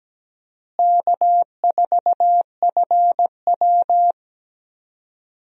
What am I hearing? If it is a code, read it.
Morse code, heard as a keyed tone, about 17 wpm, 700 Hz. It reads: K4FW
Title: USA Call Signs - 4 Characters - Repeated 2x - 17wpm